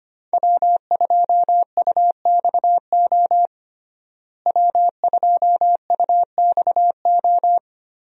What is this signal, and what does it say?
Morse code, heard as a keyed tone, about 25 wpm, 700 Hz. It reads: W2UXO W2UXO
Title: USA Call Signs - 5 Characters - Repeated 2x - 25wpm